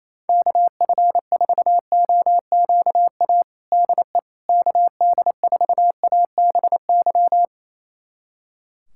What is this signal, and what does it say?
Morse code, heard as a keyed tone, about 28 wpm, 700 Hz. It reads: KF4OQA DE KD4ABY
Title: USA Callsign DE Callsign 28wpm